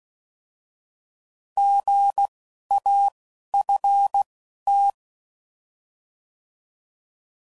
Morse code, heard as a keyed tone, about 16 wpm, 800 Hz. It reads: G A F T